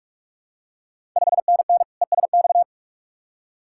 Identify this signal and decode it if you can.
Morse code, heard as a keyed tone, about 44 wpm, 690 Hz. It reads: 5NN ESX
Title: Ontario QSO Party Exchange 45wpm